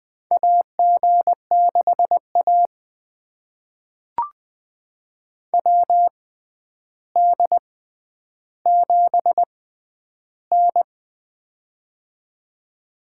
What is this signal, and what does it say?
Morse code, heard as a keyed tone, about 20 wpm, spaced as for 10 wpm, 700 Hz. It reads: AG6A WD7N
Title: USA Call Signs - 6x Character Spacing - 20wpm